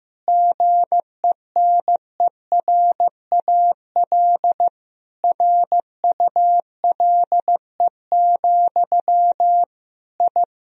Morse code, heard as a keyed tone, about 15 wpm, 700 Hz. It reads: GENERAL RULE, I
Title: Sentences from Top 1000 Words - Review Part 3 - 15wpm